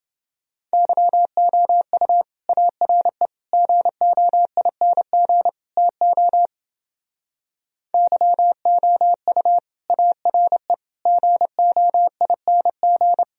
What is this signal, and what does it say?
Morse code, heard as a keyed tone, about 29 wpm, 700 Hz. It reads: YOU ARE GOING TO YOU ARE GOING